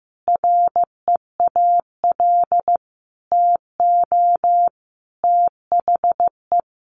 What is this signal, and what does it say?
Morse code, heard as a keyed tone, about 15 wpm, 700 Hz. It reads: REAL TO THE
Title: Sentences from Top 1400 Words 15wpm